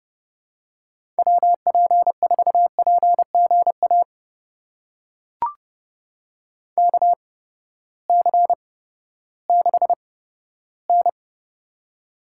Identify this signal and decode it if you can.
Morse code, heard as a keyed tone, about 30 wpm, spaced as for 15 wpm, 700 Hz. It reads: WP4PGA KC6N
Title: USA Call Signs - 8x Character Spacing - 30wpm